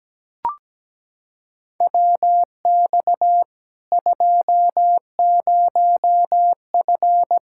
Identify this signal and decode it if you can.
Morse code, heard as a keyed tone, about 17 wpm, 700 Hz. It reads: WX 20F
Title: Practice QSO - Exchange 2 - Info - 17wpm